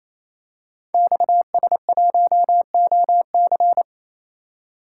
Morse code, heard as a keyed tone, about 29 wpm, 700 Hz. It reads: XS1OC
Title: Code Groups - 5 Letter:Numbers - 28wpm